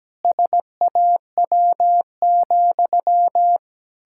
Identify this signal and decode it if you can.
Morse code, heard as a keyed tone, about 17 wpm, 700 Hz. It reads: SAW,
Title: Pangram Challenge v2 - 17wpm